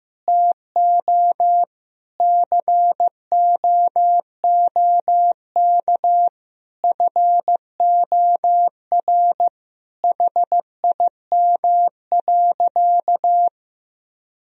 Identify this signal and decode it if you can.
Morse code, heard as a keyed tone, about 15 wpm, 700 Hz. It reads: TO COOK FOR HIM.
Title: Sentences from Top 1000 Words - Review Part 4 - 15wpm